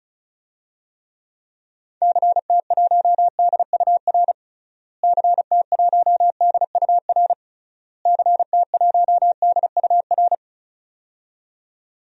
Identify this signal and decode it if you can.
Morse code, heard as a keyed tone, about 36 wpm, 700 Hz. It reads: CT1DUR CT1DUR CT1DUR
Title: DXCC-100 Call Signs - 6 Characters - Repeated 3x - 35wpm